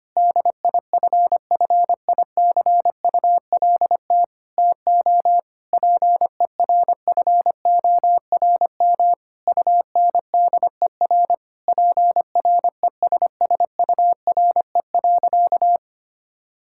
Morse code, heard as a keyed tone, about 25 wpm, 700 Hz. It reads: DIFFICULT TO PERFORM UNDER PRESSURE.
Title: Sentences from Top 1000 Words - Review Part 6 - 25wpm